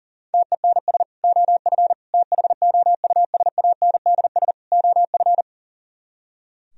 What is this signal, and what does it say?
Morse code, heard as a keyed tone, about 50 wpm, 700 Hz. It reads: TENS OF THOUSANDS OF
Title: Speed-Racing - Sets of 4 Words 40wpm